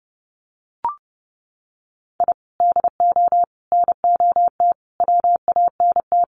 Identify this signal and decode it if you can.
Morse code, heard as a keyed tone, about 30 wpm, 700 Hz. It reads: I DO NOT WANT
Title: Sets of 4 Words 30wpm